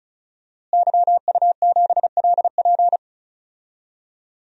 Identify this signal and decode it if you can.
Morse code, heard as a keyed tone, about 36 wpm, 700 Hz. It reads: YU7LP
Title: DXCC-100 Call Signs - 5 Characters - Repeated 2x - 35wpm